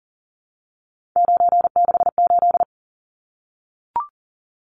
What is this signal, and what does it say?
Morse code, heard as a keyed tone, about 40 wpm, 700 Hz. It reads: 968
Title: Numbers - 3-Digit 40wpm